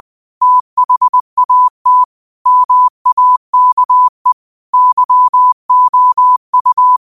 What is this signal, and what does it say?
Morse code, heard as a keyed tone, about 20 wpm, 1000 Hz. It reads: THAT MAKE YOU
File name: qotdfiles / 20wpm